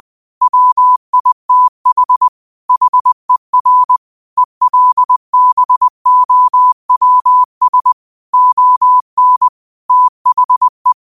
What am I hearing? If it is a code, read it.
Morse code, heard as a keyed tone, about 20 wpm, 1000 Hz. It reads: WITH HER ELBOWS ON THE